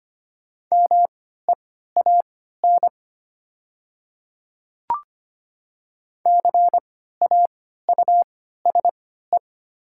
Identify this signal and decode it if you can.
Morse code, heard as a keyed tone, about 25 wpm, spaced as for 14 wpm, 700 Hz. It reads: MEAN CAUSE
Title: 3x Character Spacing - Top 300 Words - 25wpm